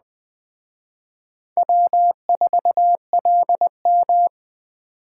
Morse code, heard as a keyed tone, about 20 wpm, 700 Hz. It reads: W4LM
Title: Speed Racing - USA 1x2 Call Signs 17wpm